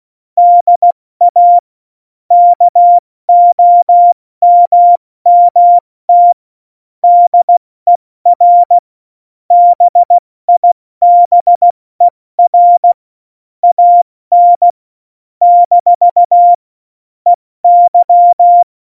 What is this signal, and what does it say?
Morse code, heard as a keyed tone, about 16 wpm, 700 Hz. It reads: DA KOMMT DER BIBER AN - EY
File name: Morsekurs / Witze / Witz 08. Dezember 2020, 70 F80